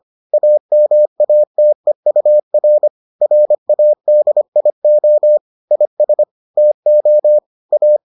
Morse code, heard as a keyed tone, about 25 wpm, 600 Hz. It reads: AMATEUR RADIO IS TO A